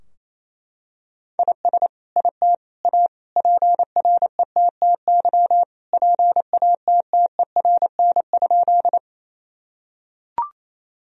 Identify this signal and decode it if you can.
Morse code, heard as a keyed tone, about 28 wpm, 700 Hz. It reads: IS IT A PRETTY PATTERN?
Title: Sentences from Top 1000 Words - Review Part 2 - 28wpm